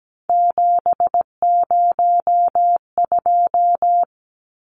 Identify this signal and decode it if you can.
Morse code, heard as a keyed tone, about 17 wpm, 700 Hz. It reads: 702
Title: Numbers - 3-Digit 17wpm